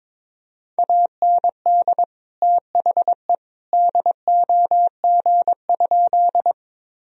Morse code, heard as a keyed tone, about 22 wpm, 700 Hz. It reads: AND THE DOG?
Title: Sentences from Top 1000 Words - Review Part 2 - 22wpm